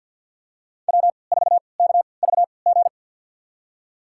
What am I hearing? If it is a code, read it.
Morse code, heard as a keyed tone, about 52 wpm, spaced as for 28 wpm, 700 Hz. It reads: W4KVG
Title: USA Call Signs - 5 Characters - 3x Character Spacing - 50wpm